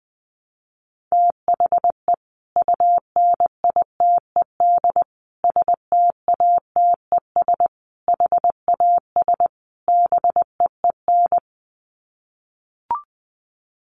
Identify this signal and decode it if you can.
Morse code, heard as a keyed tone, about 20 wpm, 700 Hz. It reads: THE UNITED STATES HAS BEEN
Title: Sets of 5 Words Encore2 20wpm